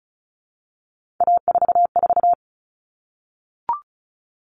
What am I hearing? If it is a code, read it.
Morse code, heard as a keyed tone, about 34 wpm, 700 Hz. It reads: A44
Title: All Cut Numbers - 3-Digit 35wpm